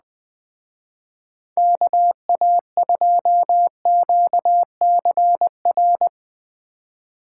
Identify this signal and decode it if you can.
Morse code, heard as a keyed tone, about 21 wpm, 700 Hz. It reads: KA2QCR